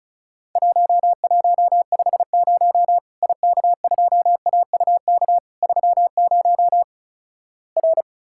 Morse code, heard as a keyed tone, about 36 wpm, 700 Hz. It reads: 1150 IK2AUK 30 R
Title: Worked All Europe DX Contest - QTC Exchange - 35wpm